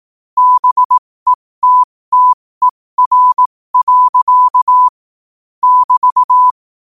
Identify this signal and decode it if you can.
Morse code, heard as a keyed tone, about 18 wpm, spaced as for 15 wpm, 1000 Hz. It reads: BETTER. <BT>
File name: qotdfiles / 15wpm